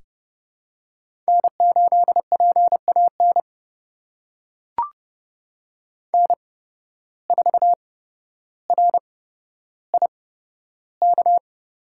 Morse code, heard as a keyed tone, about 30 wpm, spaced as for 13 wpm, 700 Hz. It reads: N8PAN N4RIK
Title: USA Call Signs - 5 Characters - 8x Character Spacing - 30wpm